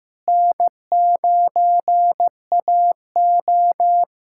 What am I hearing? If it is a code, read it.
Morse code, heard as a keyed tone, about 15 wpm, 700 Hz. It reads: N9AO
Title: USA Callsign DE Callsign - 4 Characters - 15wpm